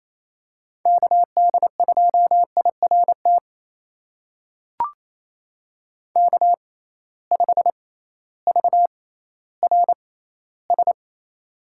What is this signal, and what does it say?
Morse code, heard as a keyed tone, about 28 wpm, spaced as for 15 wpm, 700 Hz. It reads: KD2IRT K5VRS
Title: USA Call Signs - 6x Character Spacing - 28wpm